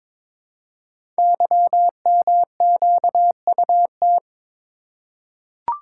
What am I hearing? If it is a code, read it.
Morse code, heard as a keyed tone, about 21 wpm, 700 Hz. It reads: YMQUT
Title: Code Groups - 5 Letters - 22wpm